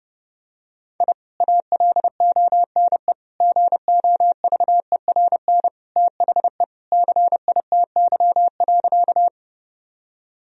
Morse code, heard as a keyed tone, about 30 wpm, 700 Hz. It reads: I ALONE GOVERN THE CITY.
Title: Sentences from Top 1000 Words - Review Part 2 - 30wpm